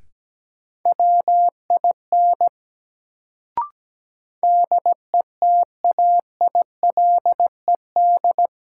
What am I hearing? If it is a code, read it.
Morse code, heard as a keyed tone, about 17 wpm, 700 Hz. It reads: WIN DETAILED
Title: Sentences from Top 1500 Words - New Word Drill Alt - 17wpm